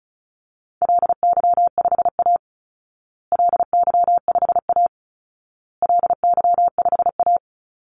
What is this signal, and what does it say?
Morse code, heard as a keyed tone, about 34 wpm, 700 Hz. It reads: LY5A LY5A LY5A
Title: Call Signs Repeated 2x - 35wpm